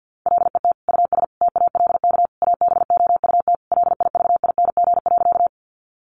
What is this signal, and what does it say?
Morse code, heard as a keyed tone, about 50 wpm, 700 Hz. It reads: LET US TALK ABOUT LIVING.
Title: Sentences from Top 1100 Words 50wpm